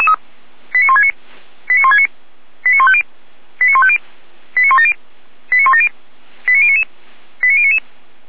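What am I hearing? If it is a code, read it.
Five-tone selective call (ZVEI), heard as quick runs of five tones, tones of about 70 ms. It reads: 89169 89169 89150 89150 89170 89170 89099 89099
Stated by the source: ZVEI five-tone call received by SDR, archived by feuerwehr-orlamuende.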